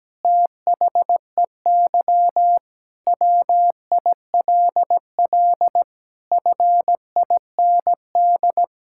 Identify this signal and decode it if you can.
Morse code, heard as a keyed tone, about 17 wpm, 700 Hz. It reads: THEY WILL FIND
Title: Sentences from Top 1000 Words - Review Part 6 - 17wpm